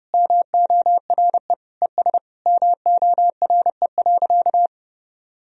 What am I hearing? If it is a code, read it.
Morse code, heard as a keyed tone, about 28 wpm, 700 Hz. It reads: MORE ES MORE.
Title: Sentences from Top 500 Words and Abbreviations 30wpm